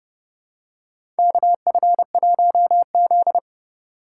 Morse code, heard as a keyed tone, about 31 wpm, 700 Hz. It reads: KF1Z
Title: USA Call Signs - 4x Character Spacing - 30wpm